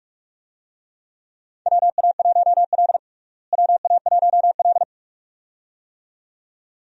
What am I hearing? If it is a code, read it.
Morse code, heard as a keyed tone, about 46 wpm, 700 Hz. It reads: WA1L WA1L
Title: USA Call Signs - 4 Characters - Repeated 2x - 45wpm